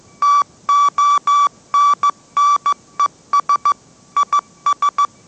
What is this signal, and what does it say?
Morse code, heard as a keyed tone, about 15 wpm, 1150 Hz. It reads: TONNES IS
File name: mrx / mrx065 16wpm